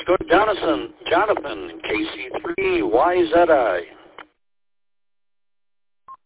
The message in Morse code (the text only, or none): none